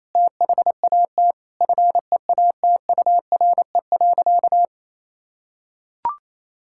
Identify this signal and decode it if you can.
Morse code, heard as a keyed tone, about 28 wpm, 700 Hz. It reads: THAT FEATURE.
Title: Sentences from Top 1000 Words - Review Part 5 - 28wpm